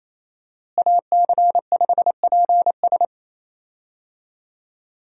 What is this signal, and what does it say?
Morse code, heard as a keyed tone, about 28 wpm, 700 Hz. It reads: AC5PS
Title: USA Call Signs - 5 Characters - Repeated 3x - 28wpm